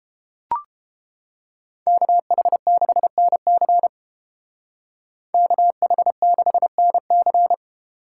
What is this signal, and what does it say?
Morse code, heard as a keyed tone, about 28 wpm, 700 Hz. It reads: KH6NC KH6NC